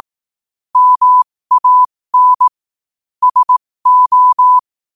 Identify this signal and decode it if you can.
Morse code, heard as a keyed tone, about 18 wpm, spaced as for 15 wpm, 1000 Hz. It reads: MAN SO
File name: qotdfiles / 15wpm